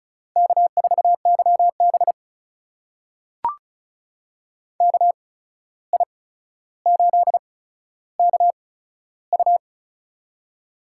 Morse code, heard as a keyed tone, about 36 wpm, spaced as for 14 wpm, 700 Hz. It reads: K4YB KI8KU
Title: USA Call Signs - 8x Character Spacing - 35wpm